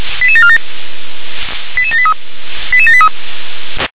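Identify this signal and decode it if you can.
Five-tone selective call (ZVEI), heard as quick runs of five tones, tones of about 70 ms. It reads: 88747 88783 88783
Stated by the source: ZVEI five-tone selective call 88702, received by SDR.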